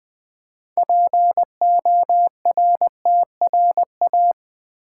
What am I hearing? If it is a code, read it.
Morse code, heard as a keyed tone, about 20 wpm, 700 Hz. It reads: PORTRA
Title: Build A Word - 2 Letters At A Time - 8-Letter Words - 20wpm